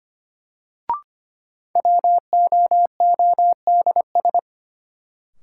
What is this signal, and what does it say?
Morse code, heard as a keyed tone, about 25 wpm, 700 Hz. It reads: WOODS
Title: Sentences from Top 1000 Words - Review Part 2 - New Word Drill - 25wpm